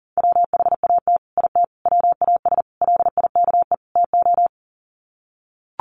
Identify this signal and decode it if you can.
Morse code, heard as a keyed tone, about 40 wpm, 700 Hz. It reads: WHAT IT WAS LIKE TO
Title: Sets of 5 Words Encore 40wpm